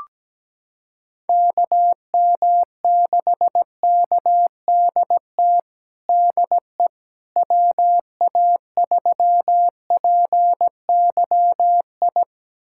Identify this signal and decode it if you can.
Morse code, heard as a keyed tone, about 17 wpm, 700 Hz. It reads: KM6KDT DE WA3PYI
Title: USA Callsign DE Callsign - 6 Characters - 17wpm